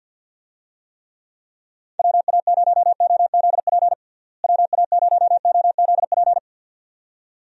Morse code, heard as a keyed tone, about 50 wpm, 700 Hz. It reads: WA0OZP WA0OZP
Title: USA Call Signs - 6 Characters - Repeated 2x - 50wpm